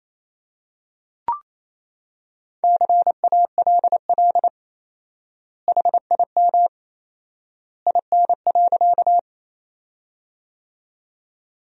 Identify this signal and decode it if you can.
Morse code, heard as a keyed tone, about 28 wpm, 700 Hz. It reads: CALL HIM IN.
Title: 4x Word-Spacing - Sentences from Top 100 Words - 28wpm